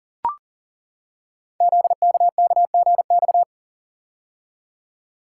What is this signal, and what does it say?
Morse code, heard as a keyed tone, about 40 wpm, 700 Hz. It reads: ZKKGX
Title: Code Groups - 5 Letter:Numbers - 40wpm